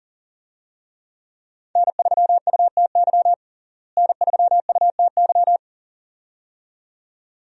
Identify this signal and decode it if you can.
Morse code, heard as a keyed tone, about 37 wpm, 700 Hz. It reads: N3UTY N3UTY
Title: USA Call Signs - 5 Characters - Repeated 2x - 40wpm